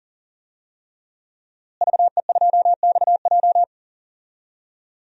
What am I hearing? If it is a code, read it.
Morse code, heard as a keyed tone, about 40 wpm, 700 Hz. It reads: VE2XJ
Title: DXCC-100 Call Signs - 5 Characters - Repeated 3x - 40wpm